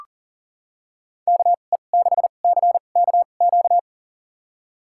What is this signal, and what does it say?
Morse code, heard as a keyed tone, about 40 wpm, spaced as for 29 wpm, 700 Hz. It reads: KE6CKQ
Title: USA Call Signs - 6 Characters - 2x Character Spacing - 40wpm